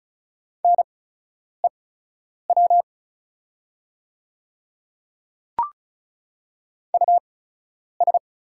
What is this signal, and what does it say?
Morse code, heard as a keyed tone, about 36 wpm, spaced as for 10 wpm, 700 Hz. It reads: NEW US
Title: 8x Character Spacing - Top 500 Words - 35wpm